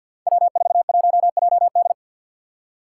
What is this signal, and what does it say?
Morse code, heard as a keyed tone, about 45 wpm, 700 Hz. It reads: W41JD
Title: Five Challenging Letter-Number Combinations 50wpm